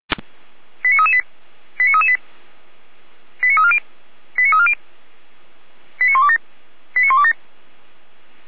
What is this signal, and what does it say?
Five-tone selective call (ZVEI), heard as quick runs of five tones, tones of about 70 ms. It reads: 89338 89338 89340 89340 89127 89127